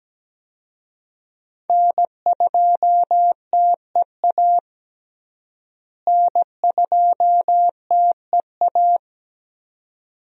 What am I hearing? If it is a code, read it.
Morse code, heard as a keyed tone, about 17 wpm, 700 Hz. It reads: N2TEA N2TEA
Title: USA Call Signs - 5 Characters - Repeated 2x - 17wpm